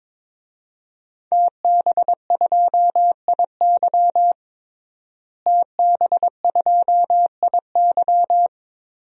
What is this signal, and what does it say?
Morse code, heard as a keyed tone, about 21 wpm, 700 Hz. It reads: TB2IY TB2IY
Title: DXCC-100 Call Signs - 5 Characters - Repeated 2x - 22wpm